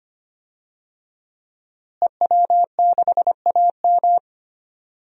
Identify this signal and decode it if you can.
Morse code, heard as a keyed tone, about 25 wpm, 700 Hz. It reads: EW6AM
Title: DXCC-100 Call Signs - 5 Characters - Repeated 3x - 25wpm